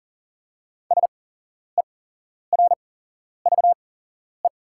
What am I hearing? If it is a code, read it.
Morse code, heard as a keyed tone, about 40 wpm, spaced as for 11 wpm, 690 Hz. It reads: SERVE
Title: 8x Character Spacing - Top 500 Words - 40wpm